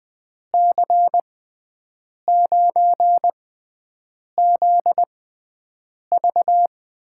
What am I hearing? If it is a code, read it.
Morse code, heard as a keyed tone, about 20 wpm, spaced as for 7 wpm, 700 Hz. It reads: C9ZV
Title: USA Call Signs - 6x Character Spacing - 20wpm